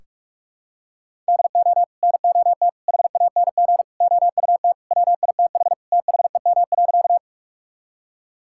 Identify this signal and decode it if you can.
Morse code, heard as a keyed tone, about 46 wpm, 700 Hz. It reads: DO NOT HANG OUT WITH THEM.
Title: Sentences from Top 1600 Words 45wpm